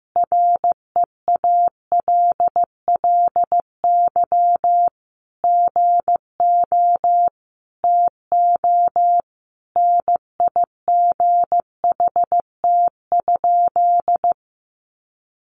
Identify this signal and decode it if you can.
Morse code, heard as a keyed tone, about 15 wpm, 700 Hz. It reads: REALLY GO TO NIGHT?
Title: Sentences from Top 600 Words 15wpm